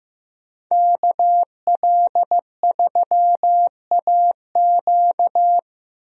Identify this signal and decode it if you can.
Morse code, heard as a keyed tone, about 15 wpm, 700 Hz. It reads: KL3AQ
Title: USA Call Signs - 5 Characters - Repeated 3x - 15wpm